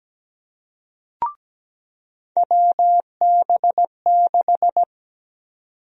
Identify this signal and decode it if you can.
Morse code, heard as a keyed tone, about 17 wpm, 700 Hz. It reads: WB6
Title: USA Call Sign Prefixes - 17wpm